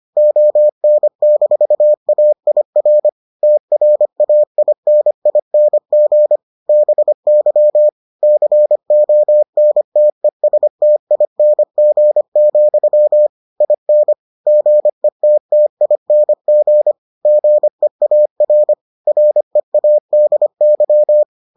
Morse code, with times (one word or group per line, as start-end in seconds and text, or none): 0.2-3.1 ON-AIR
3.4-6.4 TRAINING
6.7-7.9 BY
8.2-13.3 CONTESTING,
13.6-14.1 IN
14.5-16.9 GETTING
17.3-18.7 GEAR
19.1-21.2 READY